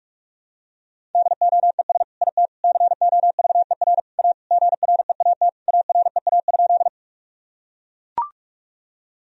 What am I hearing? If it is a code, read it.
Morse code, heard as a keyed tone, about 42 wpm, 700 Hz. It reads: DOES IT COVER A GREAT AREA?
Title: Sentences from Top 1000 Words - Review Part 2 - 45wpm